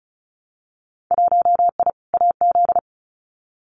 Morse code, heard as a keyed tone, about 36 wpm, 700 Hz. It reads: 1I AZ
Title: Winter Field Day Exchanges 35wpm